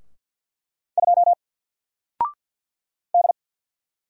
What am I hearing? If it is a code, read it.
Morse code, heard as a keyed tone, about 49 wpm, 700 Hz. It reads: 2 D
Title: Single Character (Letter:Number:Punctuation:Prosign) Alt 50wpm